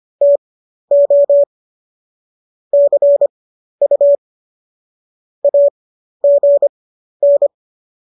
Morse code, heard as a keyed tone, about 25 wpm, spaced as for 12 wpm, 570 Hz. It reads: TO CU AGN
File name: qso012 12